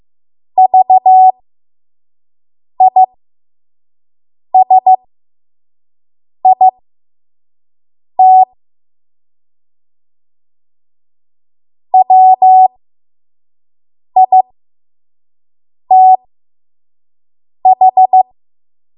Morse code, heard as a keyed tone, about 15 wpm, spaced as for 5 wpm, 750 Hz. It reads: VISIT WITH
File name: Morse / 05WPM